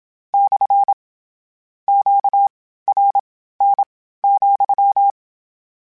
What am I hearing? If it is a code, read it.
Morse code, heard as a keyed tone, about 27 wpm, spaced as for 15 wpm, 800 Hz. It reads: / QRN,
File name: Q Calls / Q Calls-mp3s / Q Calls SPSL2 @25